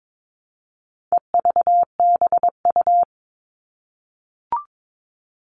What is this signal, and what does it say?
Morse code, heard as a keyed tone, about 22 wpm, 700 Hz. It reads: EVBU
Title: All Cut Numbers - 4-Digit 22wpm